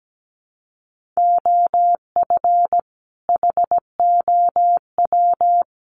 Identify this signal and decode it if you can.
Morse code, heard as a keyed tone, about 17 wpm, 700 Hz. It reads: OF HOW
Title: Sentences from Top 1400 Words 17wpm